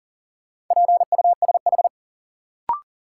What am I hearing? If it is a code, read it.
Morse code, heard as a keyed tone, about 40 wpm, 700 Hz. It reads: PUSH
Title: Sentences from Top 1300 Words - New Word Drill - 40wpm